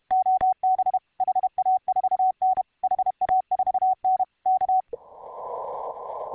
Morse code, heard as a keyed tone, about 31 wpm, 750 Hz. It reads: OB HA4N HA4N K